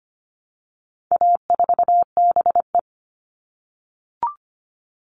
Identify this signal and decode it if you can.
Morse code, heard as a keyed tone, about 25 wpm, 700 Hz. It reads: A4BE
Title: All Cut Numbers - 4-Digit 25wpm